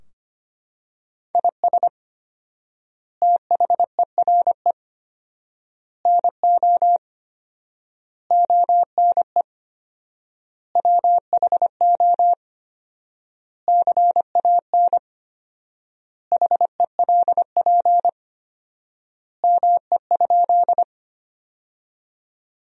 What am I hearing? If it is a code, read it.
Morse code, heard as a keyed tone, about 25 wpm, 700 Hz. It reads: IS THERE NO ONE WHO CAN HELP ME?
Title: 4x Word-Spacing - Sentences from Top 200 Words - 25wpm